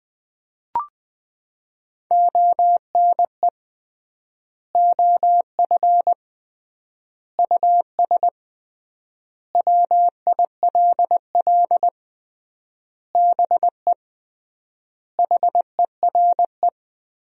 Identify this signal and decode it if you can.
Morse code, heard as a keyed tone, about 20 wpm, 700 Hz. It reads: ONE OF US WILL BE HERE
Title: 3x Word-Spacing - Sentences from Top 300 Words - 20wpm